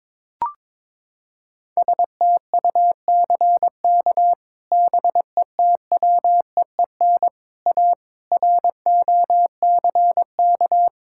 Morse code, heard as a keyed tone, about 22 wpm, 700 Hz. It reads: STUCK BETWEEN A ROCK
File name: practice / 22wpm / Sets of 8 Words - 22wpm